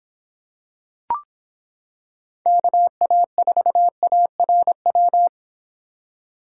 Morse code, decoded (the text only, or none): KA4ARW